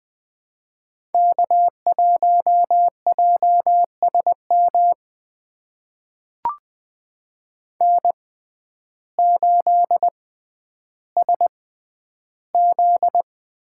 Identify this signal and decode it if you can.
Morse code, heard as a keyed tone, about 20 wpm, spaced as for 11 wpm, 700 Hz. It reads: K1JSM N8SZ
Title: USA Call Signs - 5 Characters - 6x Character Spacing - 20wpm